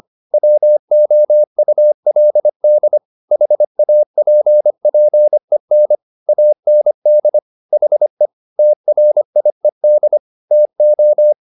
Morse code, heard as a keyed tone, about 25 wpm, 600 Hz. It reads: WOULD HAPPEN AND HE TRIED TO